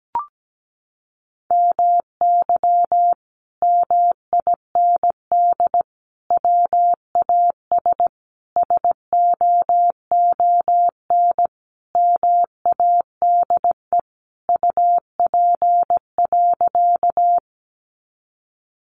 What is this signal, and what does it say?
Morse code, heard as a keyed tone, about 17 wpm, 700 Hz. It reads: MY MIND WAS SOON MADE UP.
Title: Sentences from Top 500 Words 17wpm